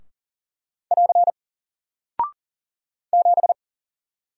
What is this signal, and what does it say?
Morse code, heard as a keyed tone, about 40 wpm, 700 Hz. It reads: <AR> 7
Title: Single Character (Letter:Number:Punctuation:Prosign) Alt 40wpm